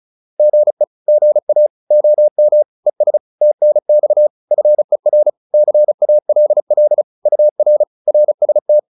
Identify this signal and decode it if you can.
Morse code, heard as a keyed tone, about 34 wpm, 600 Hz. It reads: GE GA OM ES TNX FER CALL UR RST
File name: QSO403 35